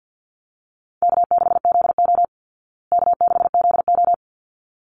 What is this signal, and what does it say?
Morse code, heard as a keyed tone, about 48 wpm, 700 Hz. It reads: K6ZO K6ZO
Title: Easy Call Signs Repeated 2x - 50wpm